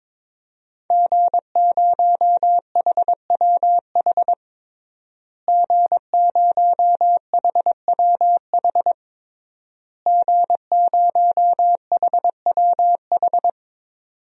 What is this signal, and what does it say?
Morse code, heard as a keyed tone, about 22 wpm, 700 Hz. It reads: G0HWH G0HWH G0HWH
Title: DXCC-100 Call Signs - 5 Characters - Repeated 3x - 22wpm